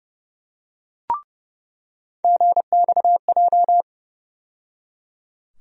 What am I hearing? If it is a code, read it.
Morse code, heard as a keyed tone, about 31 wpm, 700 Hz. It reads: GXJ
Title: USA Call Sign Postfixes - 3 Letters 30wpm